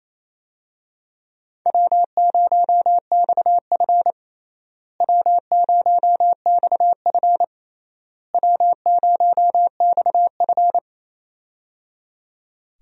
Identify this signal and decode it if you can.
Morse code, heard as a keyed tone, about 28 wpm, 700 Hz. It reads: W0XF W0XF W0XF
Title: USA Call Signs - 4 Characters - Repeated 3x - 28wpm